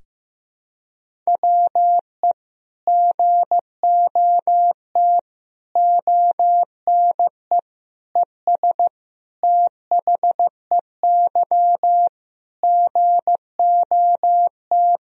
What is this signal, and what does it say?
Morse code, heard as a keyed tone, about 15 wpm, 700 Hz. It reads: WE GOT ONE ES THEY GOT
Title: Sentences from Top 300 Words and Abbreviations 15wpm